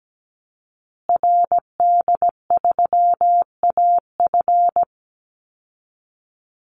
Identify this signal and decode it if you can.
Morse code, heard as a keyed tone, about 17 wpm, 700 Hz. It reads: RD3AF
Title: Call Signs Repeated 2x - 17wpm